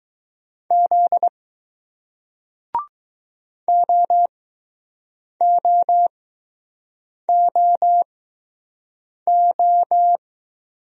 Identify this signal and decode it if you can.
Morse code, heard as a keyed tone, about 20 wpm, spaced as for 6 wpm, 700 Hz. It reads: Z OOOO